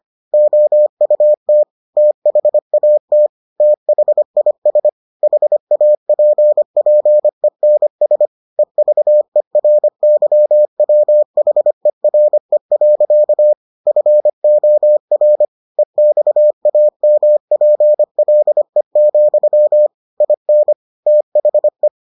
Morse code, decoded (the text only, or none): OUT THAT THIS HAPPENS EVERYWHERE. FOR EXAMPLE, IN THE